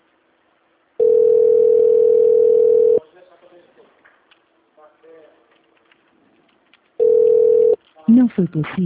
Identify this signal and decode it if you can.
Ringback tone (North America), 440 + 480 Hz, with speech.
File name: out- -2012- - - .13367